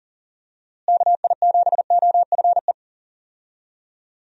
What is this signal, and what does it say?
Morse code, heard as a keyed tone, about 40 wpm, 700 Hz. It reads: KI7OFE
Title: USA Call Signs - 6 Characters - Repeated 3x - 40wpm